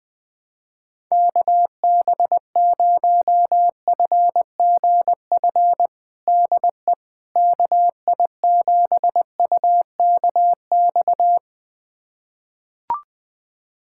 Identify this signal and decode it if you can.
Morse code, heard as a keyed tone, about 20 wpm, 700 Hz. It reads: KB0FGF DE KI7UKX
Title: USA Callsign DE Callsign - 6 Characters - 20wpm